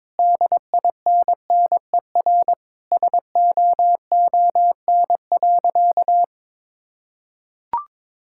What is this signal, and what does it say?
Morse code, heard as a keyed tone, about 22 wpm, 700 Hz. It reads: DINNER SOON.
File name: practice / 22wpm / Sentences with 3-Word Phrases - 22wpm